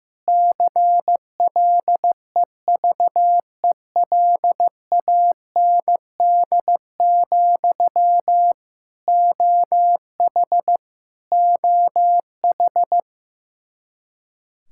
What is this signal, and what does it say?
Morse code, heard as a keyed tone, about 16 wpm, 700 Hz. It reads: CLEVELAND, OH OH
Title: K1USN SST QSO Practice 15wpm